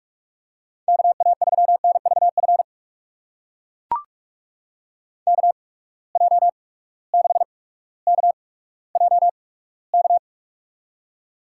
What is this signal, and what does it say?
Morse code, heard as a keyed tone, about 42 wpm, spaced as for 19 wpm, 700 Hz. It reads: KA3NVF KJ6KJK
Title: USA Call Signs - 6 Characters - 8x Character Spacing - 45wpm